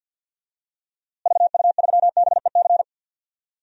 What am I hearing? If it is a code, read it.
Morse code, heard as a keyed tone, about 53 wpm, 700 Hz. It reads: VU3BEC
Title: DXCC-100 Call Signs - 6 Characters - Repeated 2x - 50wpm